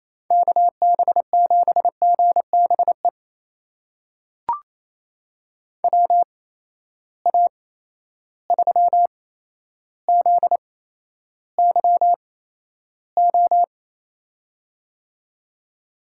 Morse code, heard as a keyed tone, about 27 wpm, spaced as for 12 wpm, 700 Hz. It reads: KB7GBE WA3ZYO
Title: USA Call Signs - 6 Characters - 8x Character Spacing - 28wpm